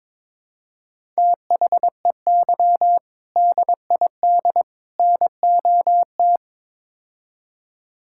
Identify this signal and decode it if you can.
Morse code, heard as a keyed tone, about 21 wpm, 700 Hz. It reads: THEY DID NOT
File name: practice / 22wpm / Build A Phrase - 5-Word Phrases - 22wpm